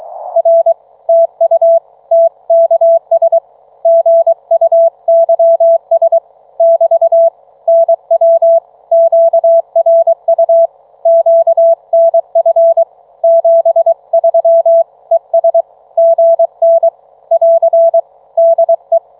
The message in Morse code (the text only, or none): R TU TKS GUYS <BT> NW QRU QNF 73 ES GN <AR> DE